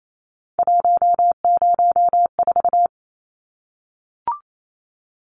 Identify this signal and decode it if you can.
Morse code, heard as a keyed tone, about 28 wpm, 700 Hz. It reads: 104
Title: Numbers - 3-Digit 28wpm